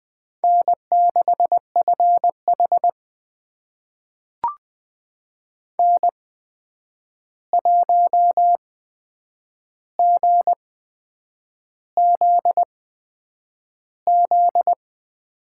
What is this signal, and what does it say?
Morse code, heard as a keyed tone, about 20 wpm, spaced as for 8 wpm, 700 Hz. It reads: N6FH N1GZZ
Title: USA Call Signs - 8x Character Spacing - 20wpm